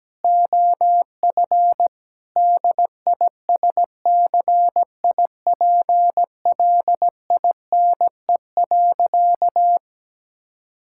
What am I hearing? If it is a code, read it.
Morse code, heard as a keyed tone, about 17 wpm, 700 Hz. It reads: OF DISCIPLINE.
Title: Sentences from Top 1000 Words - Review Part 6 - 17wpm